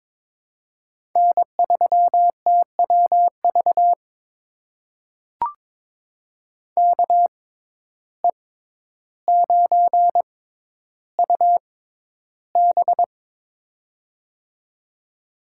Morse code, heard as a keyed tone, about 22 wpm, spaced as for 11 wpm, 700 Hz. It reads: N3TWV KE9UB
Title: USA Call Signs - 5 Characters - 6x Character Spacing - 22wpm